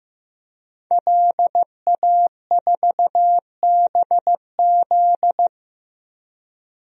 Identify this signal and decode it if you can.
Morse code, heard as a keyed tone, about 15 wpm, 700 Hz. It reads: LA4BZ
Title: DXCC-100 Call Signs - 5 Characters - Repeated 2x - 15wpm